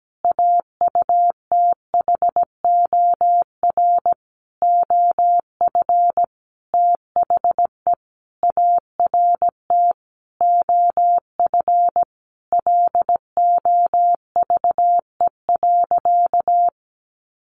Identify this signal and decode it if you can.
Morse code, heard as a keyed tone, about 17 wpm, 700 Hz. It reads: AUTHOR OF THE ART OF LOVE.
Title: Sentences from Top 1500 Words 17wpm